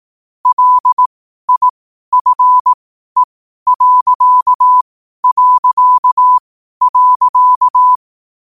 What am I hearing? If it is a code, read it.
Morse code, heard as a keyed tone, about 18 wpm, spaced as for 12 wpm, 1000 Hz. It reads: LIFE...
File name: qotdfiles / 12wpm